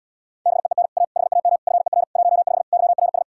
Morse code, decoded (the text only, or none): BEEN A HAM FER 15 YRS